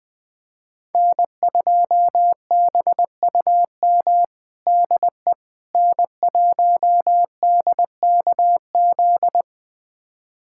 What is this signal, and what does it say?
Morse code, heard as a keyed tone, about 20 wpm, 700 Hz. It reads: N2BUM DE N1DKZ
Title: USA Callsign DE Callsign - 5 Characters - 20wpm